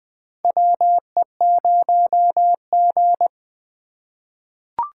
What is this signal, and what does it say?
Morse code, heard as a keyed tone, about 20 wpm, 700 Hz. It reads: WE0G
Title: USA Call Signs - 4 Characters - 3x Character Spacing - 20wpm